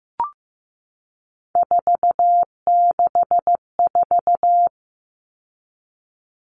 Morse code, heard as a keyed tone, about 15 wpm, 700 Hz. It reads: 464
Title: Numbers - 3-Digit 15wpm